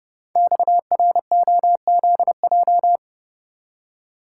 Morse code, heard as a keyed tone, about 30 wpm, 700 Hz. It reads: XROZJ
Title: Code Groups - 5 Letters - 30wpm